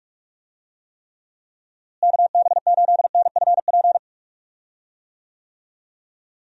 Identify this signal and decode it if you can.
Morse code, heard as a keyed tone, about 46 wpm, 700 Hz. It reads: KB8NFP
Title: USA Call Signs - 6 Characters - Repeated 2x - 45wpm